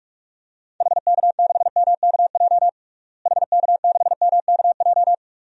Morse code, heard as a keyed tone, about 42 wpm, 700 Hz. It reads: HK6MKJ HK6MKJ
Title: DXCC-100 Call Signs - 6 Characters - Repeated 3x - 45wpm